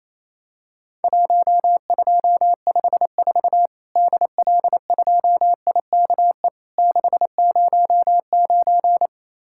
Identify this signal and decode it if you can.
Morse code, heard as a keyed tone, about 28 wpm, 700 Hz. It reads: 1254 DL2IKE 609
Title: Worked All Europe DX Contest - QTC Exchange - 28wpm